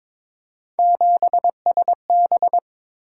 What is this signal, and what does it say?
Morse code, heard as a keyed tone, about 22 wpm, 700 Hz. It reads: 7SB
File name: practice / 22wpm / Three Letter-Number Combinations - 22wpm